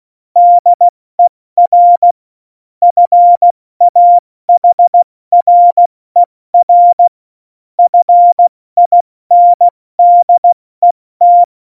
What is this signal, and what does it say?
Morse code, heard as a keyed tone, about 16 wpm, 700 Hz. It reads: DER FAHRER FINDET
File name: Morsekurs / Witze / Witz 29. Januar 2019, 70 F80